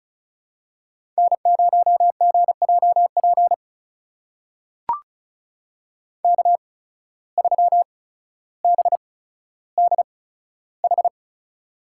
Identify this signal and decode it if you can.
Morse code, heard as a keyed tone, about 36 wpm, spaced as for 15 wpm, 700 Hz. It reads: N0GJP K3BDH
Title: USA Call Signs - 5 Characters - 8x Character Spacing - 35wpm